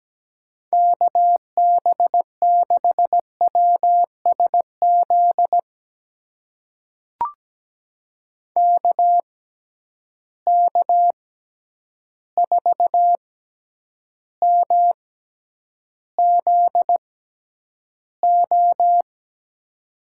Morse code, heard as a keyed tone, about 17 wpm, spaced as for 9 wpm, 700 Hz. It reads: KB6WSZ KK4MZO
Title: USA Call Signs - 6 Characters - 6x Character Spacing - 17wpm